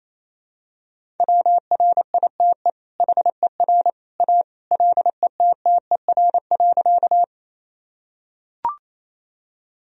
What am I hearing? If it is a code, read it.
Morse code, heard as a keyed tone, about 28 wpm, 700 Hz. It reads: WRITE HER A LETTER.
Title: Sentences from Top 1000 Words - Review Part 1 - 28wpm